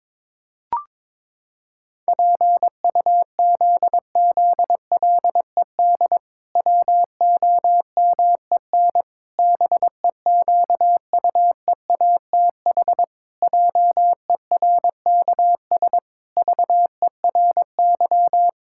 Morse code, heard as a keyed tone, about 22 wpm, 700 Hz. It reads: PUZZLED WOMEN BEQUEATH JERKS VERY
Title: Pangram Challenge v2 - 22wpm